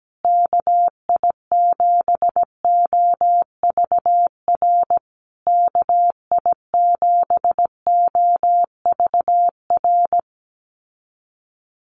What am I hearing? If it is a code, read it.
Morse code, heard as a keyed tone, about 17 wpm, 700 Hz. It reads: KI7OVR KI7OVR
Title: USA Call Signs Repeated 3x 17wpm